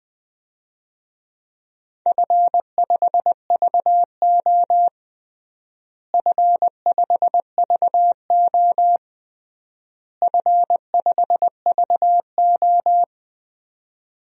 Morse code, heard as a keyed tone, about 20 wpm, 700 Hz. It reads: F5VO F5VO F5VO
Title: DXCC-100 Call Signs - 4 Characters - Repeated 3x - 20wpm